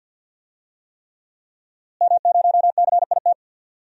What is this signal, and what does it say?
Morse code, heard as a keyed tone, about 48 wpm, 700 Hz. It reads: M0CIT
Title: DXCC-100 Call Signs - 5 Characters - Repeated 3x - 50wpm